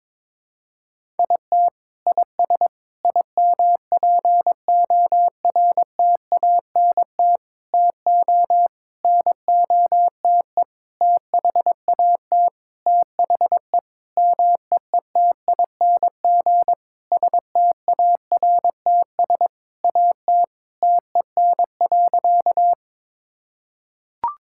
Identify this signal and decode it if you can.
Morse code, heard as a keyed tone, about 22 wpm, 700 Hz. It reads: IT IS IMPORTANT TO NOTE THAT THE MEETING STARTS AT TEN.
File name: Sentences with 7-Word Phrases - 22wpm